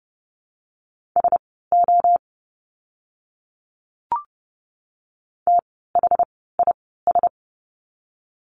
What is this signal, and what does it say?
Morse code, heard as a keyed tone, about 30 wpm, spaced as for 17 wpm, 700 Hz. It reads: SO THIS
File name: course / 30:10wpm / 230 Lesson 37f - Letter x - Words 30wpm